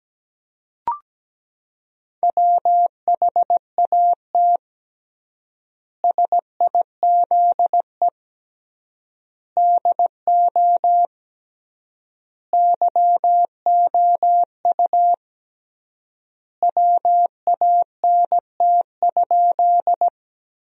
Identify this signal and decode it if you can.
Morse code, heard as a keyed tone, about 17 wpm, 700 Hz. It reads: WHAT SIZE DO YOU WANT?
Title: 3x Word-Spacing - Sentences from Top 400 Words - 17wpm